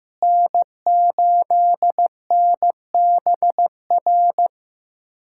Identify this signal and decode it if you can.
Morse code, heard as a keyed tone, about 15 wpm, 700 Hz. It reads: N8NBR
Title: USA Callsign DE Callsign - 5 Characters - 15wpm